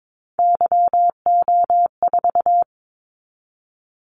Morse code, heard as a keyed tone, about 22 wpm, 700 Hz. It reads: YO4
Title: Callsign Prefixes for Top 25 DXCC - 22wpm